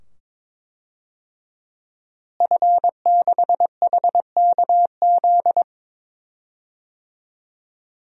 Morse code, heard as a keyed tone, about 22 wpm, 700 Hz. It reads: F6HKZ
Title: DXCC-100 Call Signs - 5 Characters - Repeated 2x - 22wpm